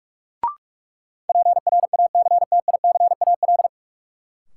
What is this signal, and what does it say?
Morse code, heard as a keyed tone, about 46 wpm, 700 Hz. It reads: PRACTICAL
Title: Sentences from Top 1400 Words - New Word Drill - 45wpm